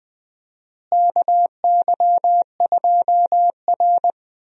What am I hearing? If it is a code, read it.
Morse code, heard as a keyed tone, about 19 wpm, 700 Hz. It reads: KY2R
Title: USA Call Signs - 4 Characters - Repeated 3x - 20wpm